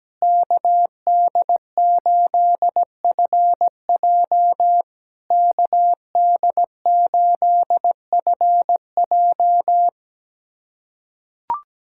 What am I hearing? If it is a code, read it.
Morse code, heard as a keyed tone, about 17 wpm, 660 Hz. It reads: KD8FJ KD8FJ
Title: SOTA QSO Practice 17wpm